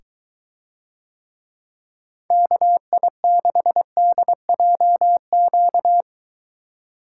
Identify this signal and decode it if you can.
Morse code, heard as a keyed tone, about 23 wpm, 700 Hz. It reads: KI6DJQ